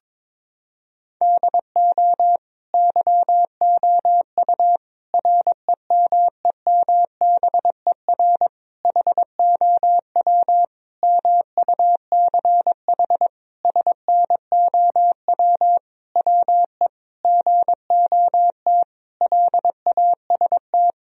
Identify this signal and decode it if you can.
Morse code, heard as a keyed tone, about 22 wpm, 700 Hz. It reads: DO YOU REMEMBER HOW MUCH SNOW WE GOT LAST
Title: Sentences from Top 1000 Words - Review Part 3 - 22wpm